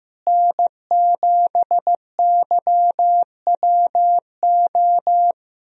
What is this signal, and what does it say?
Morse code, heard as a keyed tone, about 15 wpm, 700 Hz. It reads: N7YWO
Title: USA Call Signs - 5 Characters - Repeated 3x - 15wpm